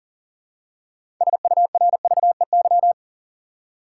Morse code, heard as a keyed tone, about 40 wpm, 700 Hz. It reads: SURVEY
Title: Six Letter Words - 40wpm